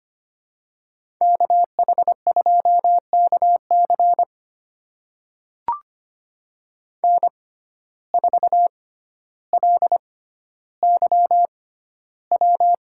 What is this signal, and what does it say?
Morse code, heard as a keyed tone, about 25 wpm, spaced as for 13 wpm, 700 Hz. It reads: KH2KC N4LYW
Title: USA Call Signs - 5 Characters - 6x Character Spacing - 25wpm